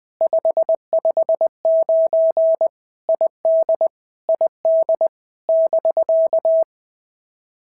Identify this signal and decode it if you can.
Morse code, heard as a keyed tone, about 18 wpm, 650 Hz. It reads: 559 ID ID <BK>
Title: POTA QSO Practice - 20wpm